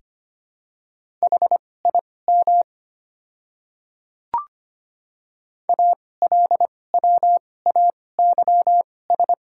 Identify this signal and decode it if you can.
Morse code, heard as a keyed tone, about 25 wpm, spaced as for 18 wpm, 700 Hz. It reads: HIM ALWAYS
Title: 2x Character Spacing - Top 300 Words - 25wpm